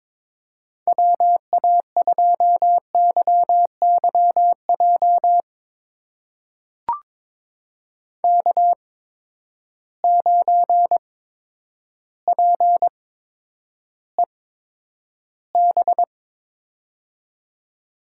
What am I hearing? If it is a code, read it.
Morse code, heard as a keyed tone, about 22 wpm, spaced as for 10 wpm, 700 Hz. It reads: WA2YYJ K9PEB
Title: USA Call Signs - 8x Character Spacing - 22wpm